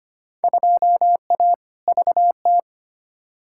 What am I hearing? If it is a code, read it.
Morse code, heard as a keyed tone, about 25 wpm, 700 Hz. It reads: 2A VT
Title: Field Day Exchanges Encore - 25wpm